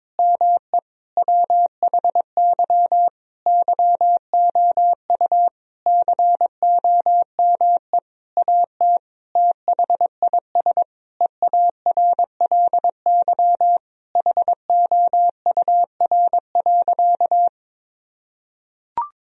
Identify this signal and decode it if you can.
Morse code, heard as a keyed tone, about 22 wpm, 700 Hz. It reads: ME WHY YOU COME AT THIS EARLY HOUR.
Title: Sentences from Top 1000 Words - Review Part 2 - 22wpm